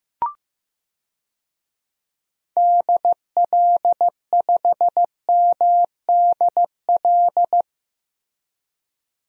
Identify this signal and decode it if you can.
Morse code, heard as a keyed tone, about 15 wpm, 700 Hz. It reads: DL5MDL
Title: DXCC-100 Call Signs - 6 Characters - Repeated 2x - 15wpm